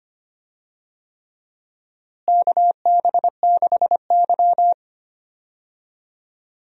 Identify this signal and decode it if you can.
Morse code, heard as a keyed tone, about 25 wpm, 700 Hz. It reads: KB6Y
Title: USA Call Signs - 4 Characters - Repeated 3x - 25wpm